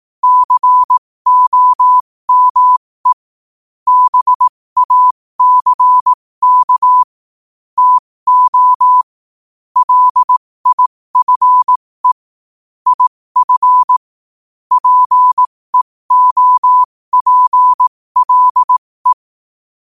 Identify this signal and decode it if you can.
Morse code, heard as a keyed tone, about 18 wpm, spaced as for 15 wpm, 1000 Hz. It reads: COME BACK TO LIFE IF PEOPLE
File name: qotdfiles / 15wpm